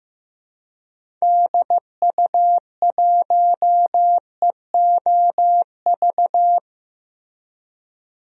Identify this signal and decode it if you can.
Morse code, heard as a keyed tone, about 15 wpm, 700 Hz. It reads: DU1EOV
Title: DXCC-100 Call Signs - 6 Characters - Repeated 2x - 15wpm